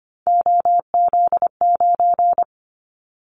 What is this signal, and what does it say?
Morse code, heard as a keyed tone, about 25 wpm, 700 Hz. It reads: OZ9
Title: Callsign Prefixes for Top 50 DXCC - 25wpm